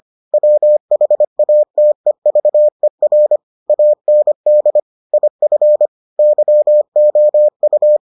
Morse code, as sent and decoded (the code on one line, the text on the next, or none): .-- .... .- - . ...- . .-. / .- -. -.. / .. ..-. / -.-- --- ..-
WHATEVER AND IF YOU